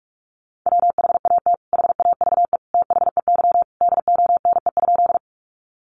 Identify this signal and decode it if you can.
Morse code, heard as a keyed tone, about 44 wpm, 700 Hz. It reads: WHAT HAVE THEY DONE?
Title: Sentences from Top 500 Words 45wpm